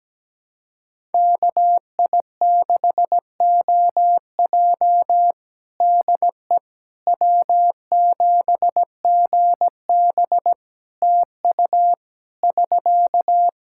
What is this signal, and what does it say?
Morse code, heard as a keyed tone, about 17 wpm, 700 Hz. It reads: KI6OJ DE W7GB TU <SK>
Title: Practice QSO - Exchange 4 - Ending - 17wpm